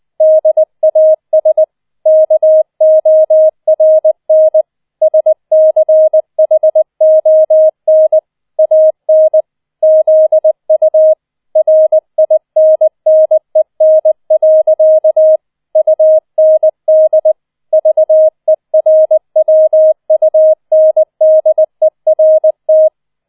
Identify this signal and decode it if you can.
Morse code, heard as a keyed tone, about 20 wpm, 630 Hz. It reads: DAS KORN SCHON AN ZU RINNEN. UND VERWUNDERT
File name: 007 Letzter Streich 20wpm